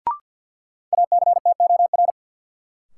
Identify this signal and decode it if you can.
Morse code, heard as a keyed tone, about 51 wpm, 700 Hz. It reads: ACTOR
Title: Sentences from Top 1500 Words - New Word Drill Alt - 50wpm